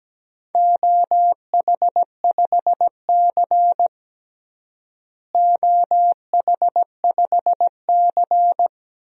Morse code, heard as a keyed tone, about 17 wpm, 700 Hz. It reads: OH5C OH5C
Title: DXCC-100 Call Signs - 4 Characters - Repeated 2x - 17wpm